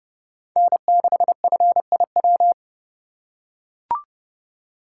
Morse code, heard as a keyed tone, about 30 wpm, 700 Hz. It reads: N6FIW